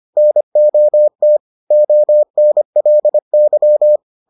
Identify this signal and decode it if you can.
Morse code, heard as a keyed tone, about 25 wpm, 600 Hz. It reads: NOT ONLY